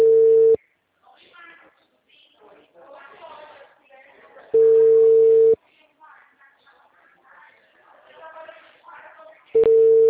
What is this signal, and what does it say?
Ringback tone (China, 450 Hz), with speech.
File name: out- -2003- - - .7458